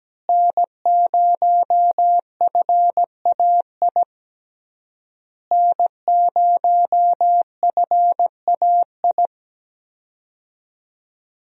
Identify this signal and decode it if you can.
Morse code, heard as a keyed tone, about 17 wpm, 700 Hz. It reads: N0FAI N0FAI
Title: USA Call Signs - 5 Characters - Repeated 3x - 17wpm